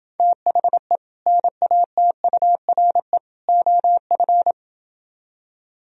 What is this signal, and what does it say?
Morse code, heard as a keyed tone, about 26 wpm, 700 Hz. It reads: THE NATURE OF